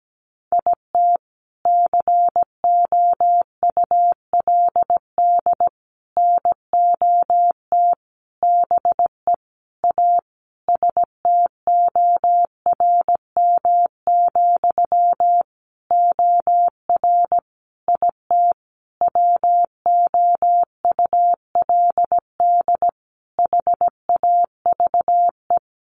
Morse code, heard as a keyed tone, about 17 wpm, 700 Hz. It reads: IT COULD NOT BE A STORM, OR IT WOULD HAVE
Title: Sentences from Top 1500 Words 17wpm